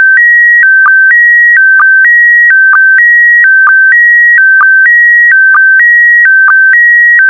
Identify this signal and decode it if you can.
SSTV (slow-scan television), PD 290.